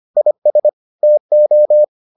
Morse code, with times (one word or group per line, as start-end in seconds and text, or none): 0.2-0.7 IS
1.0-1.8 TO